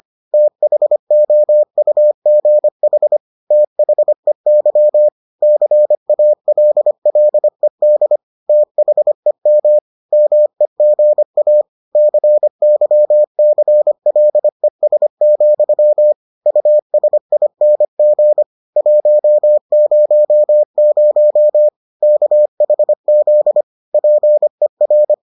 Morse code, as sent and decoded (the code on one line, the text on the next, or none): - .... --- ..- --. .... / - .... -.-- / -.-. .- .-.. .-.. -.. / - .... -- / -- . --. .- / -.-. -.-- -.-. .-.. ... --..-- / ..- ... .. -. --. / .---- ----- ----- / -.- .... --.. / .--. .-.
THOUGH THEY CALLED THEM MEGA CYCLES, USING 100 KHZ PER